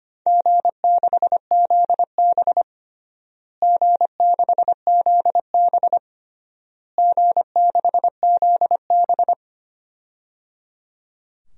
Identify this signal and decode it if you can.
Morse code, heard as a keyed tone, about 25 wpm, 700 Hz. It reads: G6ZB G6ZB G6ZB
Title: DXCC-100 Call Signs - 4 Characters - Repeated 3x - 25wpm